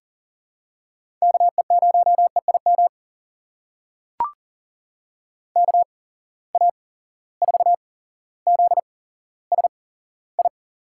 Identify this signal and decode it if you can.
Morse code, heard as a keyed tone, about 40 wpm, spaced as for 17 wpm, 700 Hz. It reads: KE0EIM KA4ZSI
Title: USA Call Signs - 6 Characters - 8x Character Spacing - 40wpm